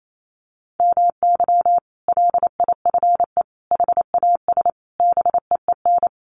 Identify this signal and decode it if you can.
Morse code, heard as a keyed tone, about 28 wpm, 700 Hz. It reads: MY LIFE HAS BEEN
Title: Sentences from Top 1100 Words 28wpm